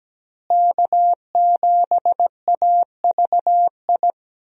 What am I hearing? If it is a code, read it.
Morse code, heard as a keyed tone, about 17 wpm, 700 Hz. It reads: K7AVI
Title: USA Call Signs - 5 Characters - Repeated 2x - 17wpm